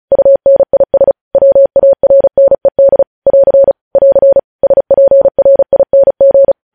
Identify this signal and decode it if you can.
Morse code, heard as a keyed tone, about 36 wpm, 550 Hz. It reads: UNIS WARNED <AR> <AR> SPRING
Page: News Headlines in Morse Code at 35 WPM